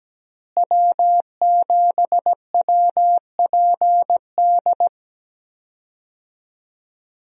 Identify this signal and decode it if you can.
Morse code, heard as a keyed tone, about 17 wpm, 700 Hz. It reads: W7WPD
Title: USA Call Signs - 5 Characters - Repeated 2x - 17wpm